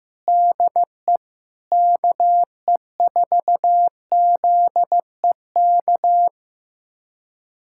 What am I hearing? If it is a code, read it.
Morse code, heard as a keyed tone, about 15 wpm, 700 Hz. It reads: DE KE4ZEK
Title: USA Callsign DE Callsign - 6 Characters - 15wpm